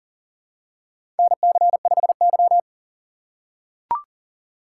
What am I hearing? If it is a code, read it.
Morse code, heard as a keyed tone, about 38 wpm, 700 Hz. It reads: NC5Y
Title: USA Call Signs - 4 Characters - 2x Character Spacing - 40wpm